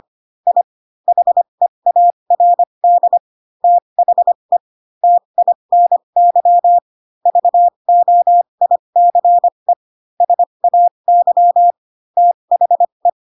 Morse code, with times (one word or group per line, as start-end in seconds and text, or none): 0.5-0.6 I
1.1-3.2 HEARD
3.6-4.6 THE
5.0-6.8 TINY
7.3-9.7 VOICE
10.2-11.7 SAY
12.2-13.1 THE